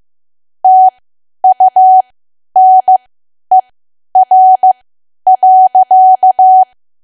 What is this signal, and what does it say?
Morse code, heard as a keyed tone, about 15 wpm, spaced as for 10 wpm, 750 Hz. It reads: TUNER.
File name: Archive / 10 WPM / 10WPM